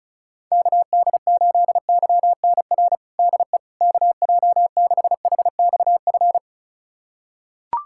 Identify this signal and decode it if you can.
Morse code, heard as a keyed tone, about 36 wpm, 700 Hz. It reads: KD8YNR DE KJ6HXF
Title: USA Callsign DE Callsign 35wpm